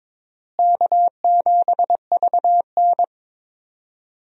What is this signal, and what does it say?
Morse code, heard as a keyed tone, about 21 wpm, 700 Hz. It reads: K7VN
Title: USA Call Signs - 4 Characters - Repeated 3x - 22wpm